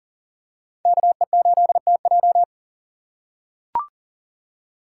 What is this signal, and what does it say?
Morse code, heard as a keyed tone, about 37 wpm, 700 Hz. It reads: KE8TJ
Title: USA Call Signs - 5 Characters - 2x Character Spacing - 40wpm